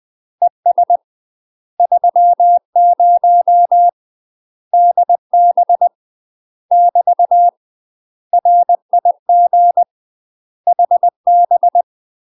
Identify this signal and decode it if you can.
Morse code, heard as a keyed tone, about 20 wpm, 700 Hz. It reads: ES 30 DB <BT> RIG HB
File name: UK MASTER QSOs TRAINING / 20WPM / UK MASTER QSOs 0039